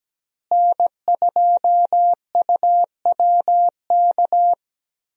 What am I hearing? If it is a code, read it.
Morse code, heard as a keyed tone, about 17 wpm, 700 Hz. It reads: N2UWK
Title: USA Call Signs - 5 Characters - Repeated 3x - 17wpm